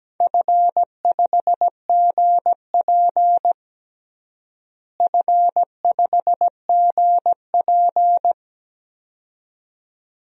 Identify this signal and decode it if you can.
Morse code, heard as a keyed tone, about 17 wpm, 700 Hz. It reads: F5GP F5GP
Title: DXCC-100 Call Signs - 4 Characters - Repeated 2x - 17wpm